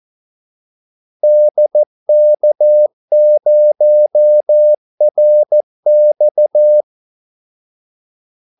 Morse code, heard as a keyed tone, about 14 wpm, 600 Hz. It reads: DK0RX
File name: Callsigns-14wpm